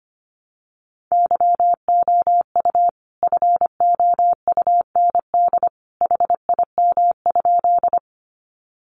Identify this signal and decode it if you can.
Morse code, heard as a keyed tone, about 25 wpm, 700 Hz. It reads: YOU FOUND HIM?
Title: Sentences from Top 300 Words 25wpm